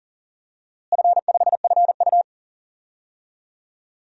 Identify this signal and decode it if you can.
Morse code, heard as a keyed tone, about 40 wpm, 700 Hz. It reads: F5FU
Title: DXCC-100 Call Signs - 4 Characters - Repeated 2x - 40wpm